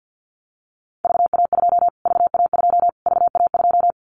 Morse code, heard as a keyed tone, about 50 wpm, 700 Hz. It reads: VA2 VA2 VA2
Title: Callsign Prefixes for Top 25 DXCC - Repeated 3x - 50wpm